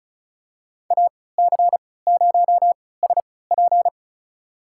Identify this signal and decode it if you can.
Morse code, heard as a keyed tone, about 36 wpm, spaced as for 20 wpm, 700 Hz. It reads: AC0SP
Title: USA Call Signs - 5 Characters - 3x Character Spacing - 35wpm